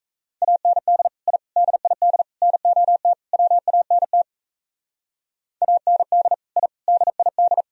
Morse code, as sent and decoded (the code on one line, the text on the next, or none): .- -. -.. / .. / -.. .. -.. / -. --- - / .-- .- -. - / .- -. -.. / .. / -.. .. -..
AND I DID NOT WANT AND I DID